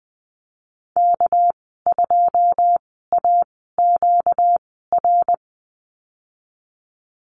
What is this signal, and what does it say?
Morse code, heard as a keyed tone, about 20 wpm, spaced as for 14 wpm, 700 Hz. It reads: K2AQR
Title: 238 Lesson 38g - Letter q - Call Signs 20wpm